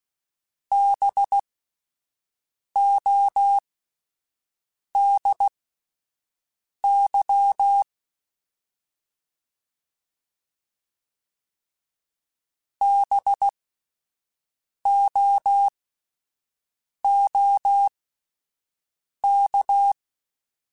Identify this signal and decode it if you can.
Morse code, heard as a keyed tone, about 16 wpm, spaced as for 5 wpm, 800 Hz. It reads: BODY BOOK
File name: Spaced Alphas / Spaced Alphas mp3s / Bs sp @10